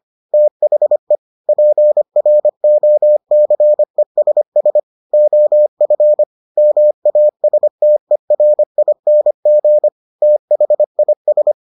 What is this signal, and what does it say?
Morse code, heard as a keyed tone, about 25 wpm, 600 Hz. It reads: THE PROCESS OF MASTERING THIS